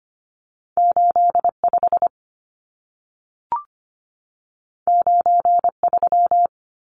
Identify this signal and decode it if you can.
Morse code, heard as a keyed tone, about 25 wpm, 700 Hz. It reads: 85 93
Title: Numbers - 2-Digit 25wpm